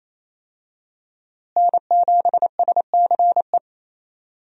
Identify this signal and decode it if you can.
Morse code, heard as a keyed tone, about 27 wpm, 700 Hz. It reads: N7SCE
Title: USA Call Signs - 5 Characters - Repeated 3x - 28wpm